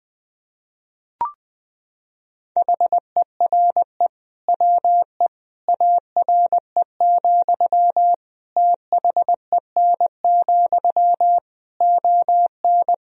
Morse code, heard as a keyed tone, about 21 wpm, 700 Hz. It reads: HERE WE ARE, THEN, ON
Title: Sentences from Top 1000 Words - Review Part 3 - 20wpm